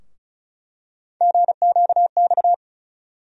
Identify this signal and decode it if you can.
Morse code, heard as a keyed tone, about 34 wpm, 700 Hz. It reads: GQX
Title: USA Call Sign Postfixes - 3 Letters 35wpm